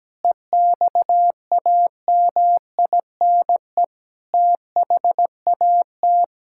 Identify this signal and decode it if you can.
Morse code, heard as a keyed tone, about 17 wpm, 700 Hz. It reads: EXAMINE THAT
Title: Sentences from Top 1600 Words 17wpm